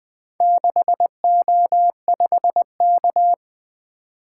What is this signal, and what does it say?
Morse code, heard as a keyed tone, about 20 wpm, 700 Hz. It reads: 6O5K
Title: Four Letter-Number Combinations - 20wpm